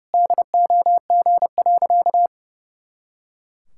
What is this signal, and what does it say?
Morse code, heard as a keyed tone, about 32 wpm, 700 Hz. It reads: DOG.
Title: Pangram Challenge v2 - 30wpm